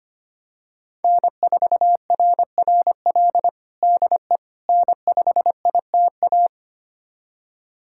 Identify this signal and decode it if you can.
Morse code, heard as a keyed tone, about 25 wpm, 700 Hz. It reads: N4RRL DE N5ITA
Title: USA Callsign DE Callsign - 5 Characters - 25wpm